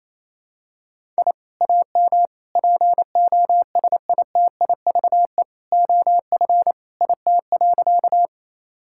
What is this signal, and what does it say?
Morse code, heard as a keyed tone, about 28 wpm, 700 Hz. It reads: I AM POSITIVE OF IT.
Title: Sentences from Top 1000 Words - Review Part 5 - 28wpm